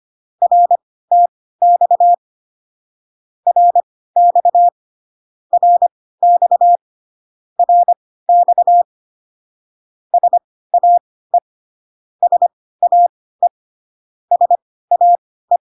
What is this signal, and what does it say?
Morse code, heard as a keyed tone, about 25 wpm, spaced as for 16 wpm, 700 Hz. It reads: RTX RX RX RX SAE SAE SAE